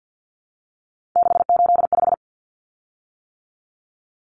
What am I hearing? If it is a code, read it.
Morse code, heard as a keyed tone, about 50 wpm, 700 Hz. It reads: 685
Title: Common Cut Numbers - 3-Digit 50wpm